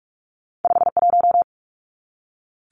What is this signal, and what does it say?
Morse code, heard as a keyed tone, about 46 wpm, 700 Hz. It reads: 51
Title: Numbers - 2-Digit 45wpm